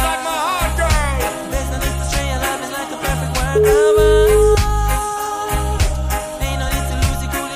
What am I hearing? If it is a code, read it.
Ringback tone (Europe, 425 Hz), with noise.